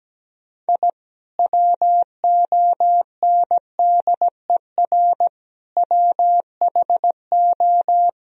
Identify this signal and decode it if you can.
Morse code, heard as a keyed tone, about 17 wpm, 700 Hz. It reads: I WONDER WHO
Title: Sentences from Top 1000 Words - Review Part 3 - 17wpm